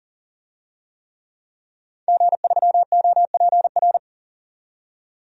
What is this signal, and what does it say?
Morse code, heard as a keyed tone, about 39 wpm, 700 Hz. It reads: G3OPR
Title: DXCC-100 Call Signs - 5 Characters - 40wpm